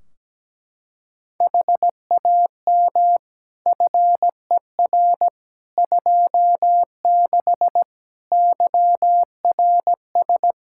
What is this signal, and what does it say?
Morse code, heard as a keyed tone, about 17 wpm, 700 Hz. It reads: HAM FER 26 YRS
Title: Practice QSO - Exchange 3 - More Info - 17wpm